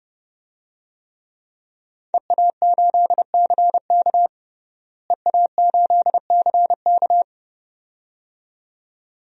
Morse code, heard as a keyed tone, about 30 wpm, 700 Hz. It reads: EA8CK EA8CK
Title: DXCC-100 Call Signs - 5 Characters - Repeated 2x - 30wpm